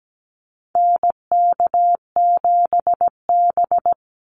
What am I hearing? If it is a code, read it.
Morse code, heard as a keyed tone, about 17 wpm, 700 Hz. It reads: NK7B
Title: Call Signs Repeated 2x - 17wpm